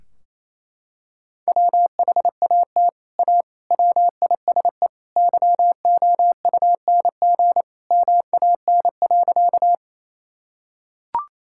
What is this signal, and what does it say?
Morse code, heard as a keyed tone, about 28 wpm, 700 Hz. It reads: WHAT A WISE YOUNG MAN.
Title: Sentences from Top 1600 Words 28wpm